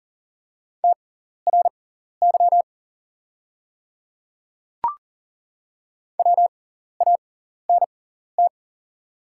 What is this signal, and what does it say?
Morse code, heard as a keyed tone, about 40 wpm, spaced as for 14 wpm, 700 Hz. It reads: TRY WANT
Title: 6x Character Spacing - Top 200 Words - 40wpm